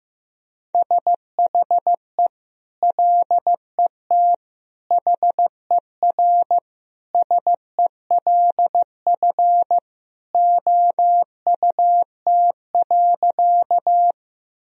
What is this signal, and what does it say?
Morse code, heard as a keyed tone, about 15 wpm, 700 Hz. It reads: SHE LET HERSELF OUT.
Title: Sentences from Top 1000 Words - Review Part 1 - 15wpm